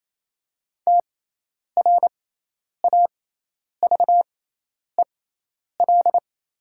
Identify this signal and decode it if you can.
Morse code, heard as a keyed tone, about 28 wpm, spaced as for 10 wpm, 700 Hz. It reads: TRAVEL
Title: 6x Character Spacing - Top 500 Words - 28wpm